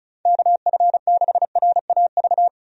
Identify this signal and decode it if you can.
Morse code, heard as a keyed tone, about 34 wpm, 700 Hz. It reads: KF6RAV
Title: USA Call Signs - 6 Characters - 35wpm